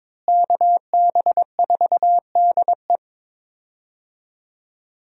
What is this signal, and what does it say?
Morse code, heard as a keyed tone, about 21 wpm, 700 Hz. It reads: KB4DE
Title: USA Call Signs - 5 Characters - Repeated 3x - 22wpm